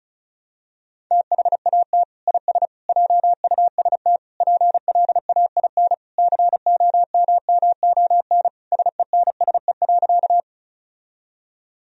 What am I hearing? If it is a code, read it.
Morse code, heard as a keyed tone, about 36 wpm, 700 Hz. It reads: THAT IS JUST PLAIN COMMON SENSE.
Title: Sentences from Top 1000 Words - Review Part 4 - 35wpm